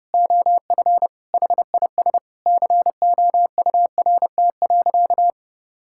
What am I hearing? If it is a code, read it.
Morse code, heard as a keyed tone, about 32 wpm, 700 Hz. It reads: OF HIS COURT.
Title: Sentences from Top 1600 Words 30wpm